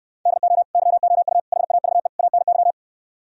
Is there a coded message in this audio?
Yes: Morse code from 0.3 to 2.7 s, heard as a keyed tone, about 67 wpm, 700 Hz. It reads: DO YOU HAVE ANY